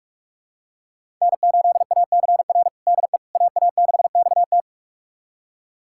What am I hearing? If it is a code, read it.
Morse code, heard as a keyed tone, about 41 wpm, 700 Hz. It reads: N8ACR DE AA6XT
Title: USA Callsign DE Callsign 45wpm